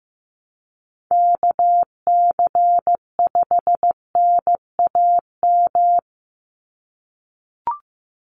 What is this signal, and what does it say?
Morse code, heard as a keyed tone, about 15 wpm, 700 Hz. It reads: KC5NAM
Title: USA Call Signs 15wpm